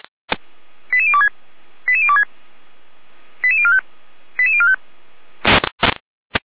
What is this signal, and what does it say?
Five-tone selective call (ZVEI), heard as quick runs of five tones, tones of about 70 ms. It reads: 88026 88026 88045 88045